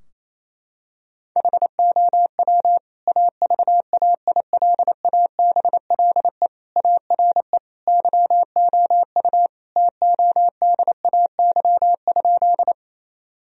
Morse code, heard as a keyed tone, about 28 wpm, 700 Hz. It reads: HOW AVAILABLE ARE YOU TODAY?
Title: Sentences from Top 1000 Words - Review Part 3 - 28wpm